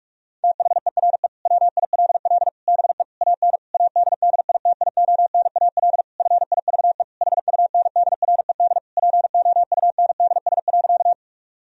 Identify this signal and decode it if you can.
Morse code, heard as a keyed tone, about 46 wpm, 700 Hz. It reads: THERE WILL BE AN ADDITIONAL FIVE HUNDRED POUNDS.
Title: Sentences from Top 1000 Words - Review Part 4 - 45wpm